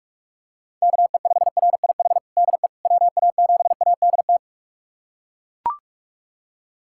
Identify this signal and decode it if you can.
Morse code, heard as a keyed tone, about 41 wpm, 700 Hz. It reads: KE5RIH DE WA7ADT
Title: USA Callsign DE Callsign - 6 Characters - 45wpm